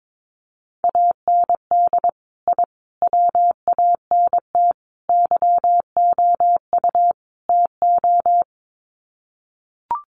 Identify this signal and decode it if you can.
Morse code, heard as a keyed tone, about 22 wpm, 700 Hz. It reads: AND I WANT YOU TO
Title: Sets of 5 Words Encore 22wpm